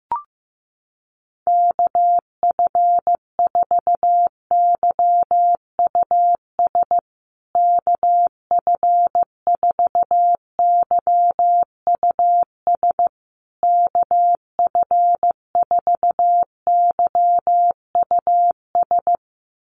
Morse code, heard as a keyed tone, about 15 wpm, 700 Hz. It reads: KF4YUS KF4YUS KF4YUS
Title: USA Call Signs Repeated 3x 15wpm